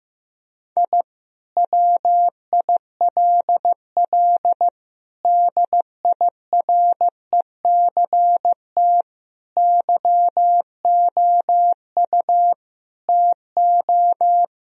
Morse code, heard as a keyed tone, about 15 wpm, 700 Hz. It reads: I WILL DIRECT YOU TO
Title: Sentences from Top 1000 Words - Review Part 2 - 15wpm